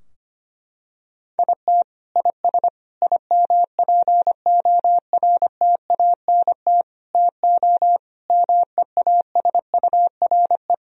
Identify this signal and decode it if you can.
Morse code, heard as a keyed tone, about 25 wpm, 700 Hz. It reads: IT IS IMPORTANT TO MEASURE
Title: Sentences from Top 1000 Words - Review Part 2 - 25wpm